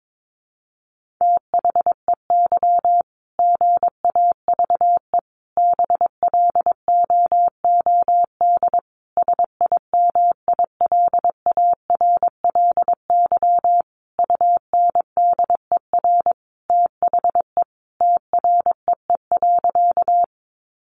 Morse code, heard as a keyed tone, about 22 wpm, 700 Hz. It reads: THEY GAVE BLOOD SIMILARLY UNDER THE TREE.
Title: Sentences from Top 1400 Words 22wpm